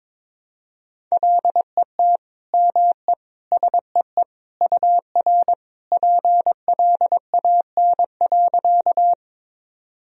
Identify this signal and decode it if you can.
Morse code, heard as a keyed tone, about 22 wpm, 700 Hz. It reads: LET ME SEE UR PLAN.
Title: Sentences from Top 500 Words and Abbreviations 22wpm